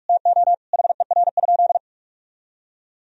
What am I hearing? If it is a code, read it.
Morse code, heard as a keyed tone, about 46 wpm, 700 Hz. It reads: TO HER?
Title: Sentences from Top 1000 Words - Review Part 4 - 45wpm